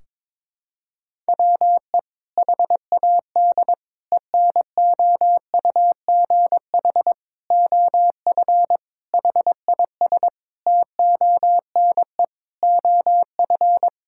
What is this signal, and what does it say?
Morse code, heard as a keyed tone, about 22 wpm, 700 Hz. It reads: WE HAD ENOUGH OF HIS TONE OF
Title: Sentences from Top 1600 Words 22wpm